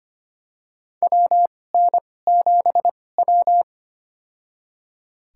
Morse code, heard as a keyed tone, about 25 wpm, spaced as for 18 wpm, 700 Hz. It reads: WN7W
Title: USA Call Signs - 4 Characters - 2x Character Spacing - 25wpm